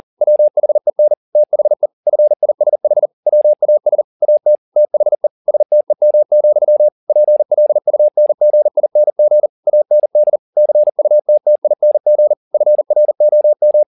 Morse code, heard as a keyed tone, about 40 wpm, 600 Hz. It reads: WHEN THE FISH WAS AT THE STEM, PLUNGING AND CUTTING FROM
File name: the old man and the sea 2 40wpm